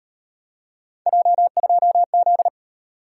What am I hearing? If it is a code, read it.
Morse code, heard as a keyed tone, about 37 wpm, 700 Hz. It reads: J2Z